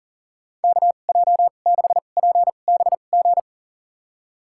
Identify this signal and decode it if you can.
Morse code, heard as a keyed tone, about 40 wpm, spaced as for 29 wpm, 700 Hz. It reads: KJ6PBG
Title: USA Call Signs - 6 Characters - 2x Character Spacing - 40wpm